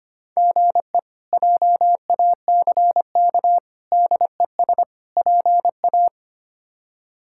Morse code, heard as a keyed tone, about 25 wpm, 700 Hz. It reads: GE JACK DES PA
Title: K1USN SST QSO Practice - 25wpm